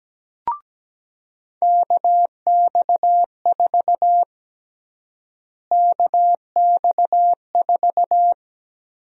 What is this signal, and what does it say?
Morse code, heard as a keyed tone, about 17 wpm, 700 Hz. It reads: KX4 KX4
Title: USA Call Sign Prefixes - Repeated 2x - 17wpm